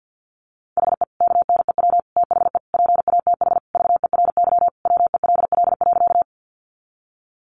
Morse code, heard as a keyed tone, about 50 wpm, 700 Hz. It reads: HE KNEW THE PATH VERY WELL.
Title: Sentences from Top 1400 Words 50wpm